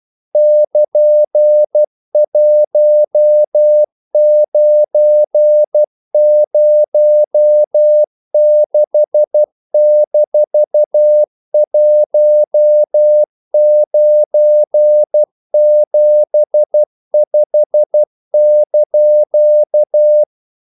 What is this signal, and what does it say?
Morse code, heard as a keyed tone, about 12 wpm, 600 Hz. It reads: <KN>1906-1975)
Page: English quotations in CW (telegraphy - morse)